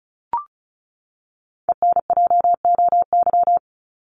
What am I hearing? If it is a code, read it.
Morse code, heard as a keyed tone, about 34 wpm, 700 Hz. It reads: ENJOY
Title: Top 751-1000 Words 35wpm